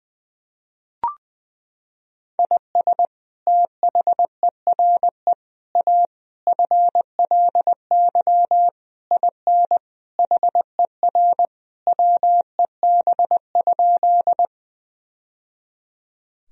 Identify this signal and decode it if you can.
Morse code, heard as a keyed tone, about 20 wpm, 700 Hz. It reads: IS THERE A FLY IN HER WEB?
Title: Sentences from Top 1000 Words - Review Part 4 - 20wpm